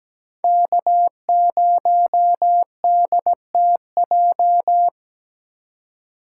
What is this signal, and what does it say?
Morse code, heard as a keyed tone, about 22 wpm, 700 Hz. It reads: K0DTJ